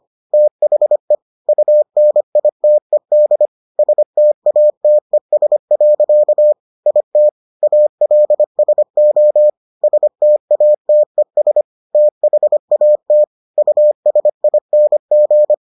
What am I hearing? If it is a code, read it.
Morse code, heard as a keyed tone, about 25 wpm, 600 Hz. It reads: THE UNITED STATES. IT ALSO STATES THAT USING